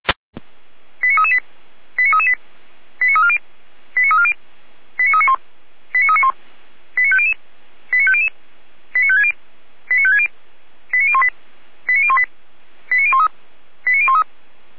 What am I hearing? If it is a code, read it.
Five-tone selective call (ZVEI), heard as quick runs of five tones, tones of about 70 ms. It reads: 89338 89338 89340 89340 89391 89391 89500 89500 89560 89560 89018 89018 89013 89013
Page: /455/SDR/ZVEI/89330/